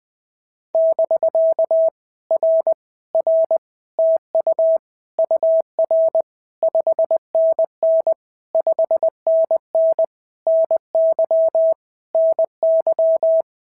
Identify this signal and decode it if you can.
Morse code, heard as a keyed tone, about 19 wpm, 650 Hz. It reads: <BK> R R TU UR 5NN 5NN NY NY
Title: POTA QSO Practice - 20wpm